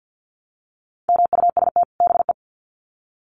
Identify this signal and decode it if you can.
Morse code, heard as a keyed tone, about 50 wpm, 700 Hz. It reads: MUST BE
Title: Sets of 2 Words 50wpm